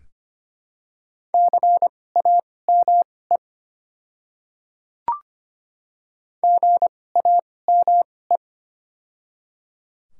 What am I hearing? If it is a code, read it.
Morse code, heard as a keyed tone, about 25 wpm, spaced as for 18 wpm, 700 Hz. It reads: CAME GAME
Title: 2x Character Spacing - Top 500 Words - 25wpm